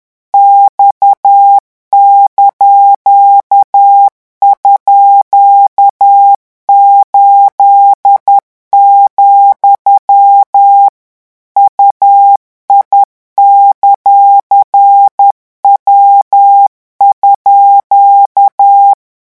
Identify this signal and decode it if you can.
Morse code, heard as a keyed tone, about 10 wpm, 800 Hz. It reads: X)_8, UI;W_
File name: morse mp3 / revision17-vitesse 10 mots